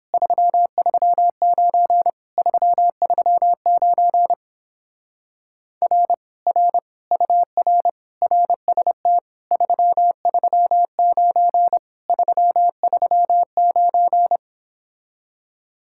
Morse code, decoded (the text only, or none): 339 339 R R UR RST 339 339